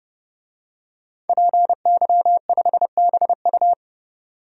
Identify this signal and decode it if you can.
Morse code, heard as a keyed tone, about 31 wpm, 700 Hz. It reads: PY5BU
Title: DXCC-100 Call Signs - 5 Characters - 30wpm